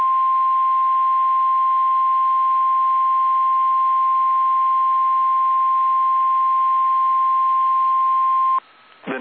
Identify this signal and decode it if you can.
Emergency Alert System (SAME) tones (the 1050 Hz attention tone). It is heard from 0.5 to 8.3 s.
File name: eas / KSDZKDJL / KSDZKDJL